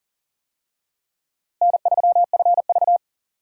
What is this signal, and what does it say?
Morse code, heard as a keyed tone, about 39 wpm, 700 Hz. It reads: N3FV
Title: USA Call Signs - 4 Characters - Repeated 3x - 40wpm